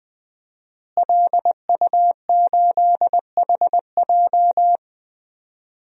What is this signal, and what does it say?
Morse code, heard as a keyed tone, about 21 wpm, 700 Hz. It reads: LU8HJ
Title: DXCC-100 Call Signs - 5 Characters - Repeated 3x - 20wpm